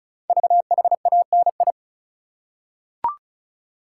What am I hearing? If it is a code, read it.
Morse code, heard as a keyed tone, about 36 wpm, 690 Hz. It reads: VHANI
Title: Code Groups - 5 Letters - 35wpm